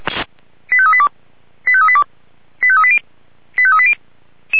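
Five-tone selective call (ZVEI), heard as quick runs of five tones, tones of about 70 ms. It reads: 85382 85382 85380 85380
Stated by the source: ZVEI five-tone selective call received on SDR.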